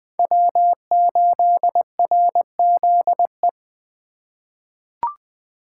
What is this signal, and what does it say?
Morse code, heard as a keyed tone, about 20 wpm, 700 Hz. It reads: W8RZE
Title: USA Call Signs - 5 Characters - 4x Character Spacing - 20wpm